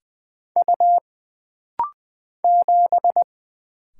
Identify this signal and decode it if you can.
Morse code, heard as a keyed tone, about 20 wpm, 700 Hz. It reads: U 7
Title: Single Letter-Number Alt - Rapid-Fire 20wpm